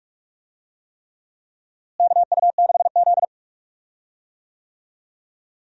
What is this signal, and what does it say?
Morse code, heard as a keyed tone, about 45 wpm, 700 Hz. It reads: KU6Z
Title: USA Call Signs - 4 Characters - Repeated 3x - 45wpm